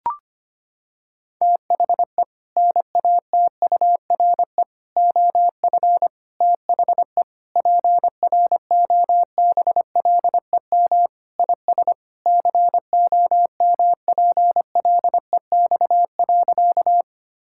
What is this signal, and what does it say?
Morse code, heard as a keyed tone, about 25 wpm, 700 Hz. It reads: THE NATURE OF THE PROBLEM IS COMPLEX.
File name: practice / 25wpm / Sentences with 4-Word Phrases - 25wpm